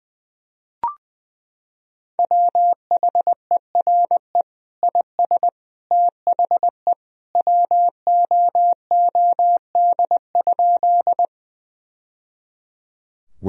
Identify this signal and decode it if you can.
Morse code, heard as a keyed tone, about 21 wpm, 700 Hz. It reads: WHERE IS THE WOOD?
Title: Sentences from Top 1000 Words - Review Part 2 - 20wpm